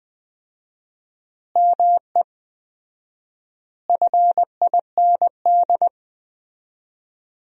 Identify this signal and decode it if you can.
Morse code, heard as a keyed tone, about 19 wpm, 700 Hz. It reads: ME FIND
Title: 4x Word-Spacing - Sentences from Top 200 Words - 20wpm